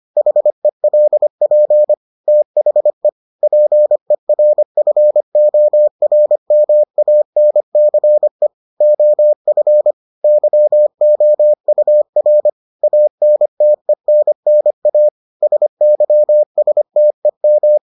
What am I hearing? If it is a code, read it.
Morse code, heard as a keyed tone, about 25 wpm, 600 Hz. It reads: HELP THE PERFORMANCE OF YOUR ANTENNA SYSTEM